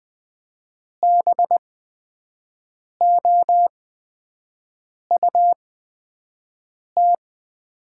Morse code, heard as a keyed tone, about 20 wpm, spaced as for 5 wpm, 700 Hz. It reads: BOUT
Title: 8x Character Spacing - Top 200 Words - 20wpm